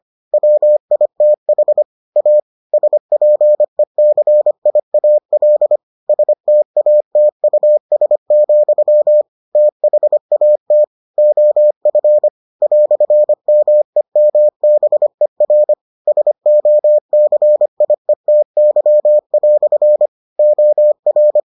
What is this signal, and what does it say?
Morse code, heard as a keyed tone, about 25 wpm, 600 Hz. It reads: WITH A SPECIAL STATUS, THAT OF "MEMBER SOCIETY" OR